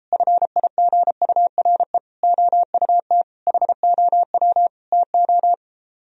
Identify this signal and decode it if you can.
Morse code, heard as a keyed tone, about 33 wpm, 700 Hz. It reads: FIGURE OUT HOW TO